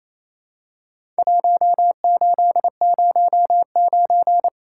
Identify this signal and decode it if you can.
Morse code, heard as a keyed tone, about 28 wpm, 700 Hz. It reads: 1809
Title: Worked All Europe DX Contest - QTC Exchange - 28wpm